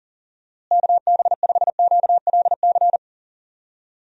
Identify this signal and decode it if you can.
Morse code, heard as a keyed tone, about 40 wpm, 700 Hz. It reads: KB5QLC
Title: USA Call Signs - 6 Characters - Repeated 2x - 40wpm